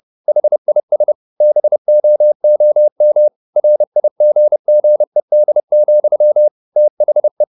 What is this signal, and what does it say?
Morse code, heard as a keyed tone, about 30 wpm, 600 Hz. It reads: HIS BOOM RIGGED, THE